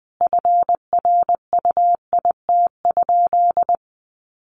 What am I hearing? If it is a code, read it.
Morse code, heard as a keyed tone, about 20 wpm, 700 Hz. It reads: FRUIT?
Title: Sentences from Top 1500 Words 20wpm